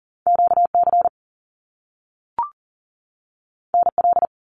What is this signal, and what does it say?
Morse code, heard as a keyed tone, about 40 wpm, 700 Hz. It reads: QC NL
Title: Canadian Provinces and Territories 40wpm